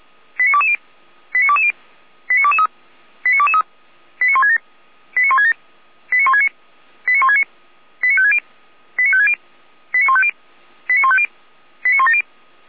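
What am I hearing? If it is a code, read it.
Five-tone selective call (ZVEI), heard as quick runs of five tones, tones of about 70 ms. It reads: 89220 89220 89223 89223 89167 89167 89169 89169 89560 89560 89150 89150 89170
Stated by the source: ZVEI five-tone selective call received on SDR, channel 455, code 89022, from audio.f85360.feuerwehr-orlamuende.org fire brigade server.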